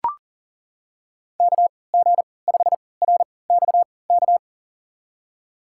Morse code, heard as a keyed tone, about 40 wpm, spaced as for 23 wpm, 690 Hz. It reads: KG5RXK
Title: USA Call Signs - 3x Character Spacing - 40wpm